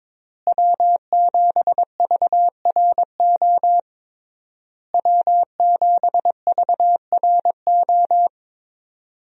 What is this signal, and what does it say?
Morse code, heard as a keyed tone, about 22 wpm, 700 Hz. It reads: W7VRO W7VRO
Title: USA Call Signs - 5 Characters - Repeated 3x - 22wpm